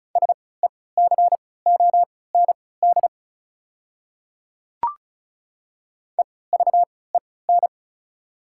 Morse code, heard as a keyed tone, about 34 wpm, spaced as for 20 wpm, 690 Hz. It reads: SECOND EVEN
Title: 3x Character Spacing - Top 300 Words - 35wpm